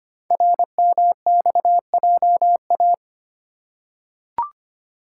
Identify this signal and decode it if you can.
Morse code, heard as a keyed tone, about 25 wpm, 700 Hz. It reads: RMXJA
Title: Code Groups - 5 Letters - 25wpm